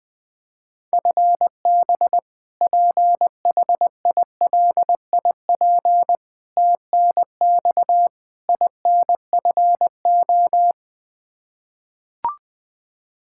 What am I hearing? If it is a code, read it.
Morse code, heard as a keyed tone, about 21 wpm, 700 Hz. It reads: FB PHILIP TNX INFO
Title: Practice QSO - Exchange 3 - More Info - 20wpm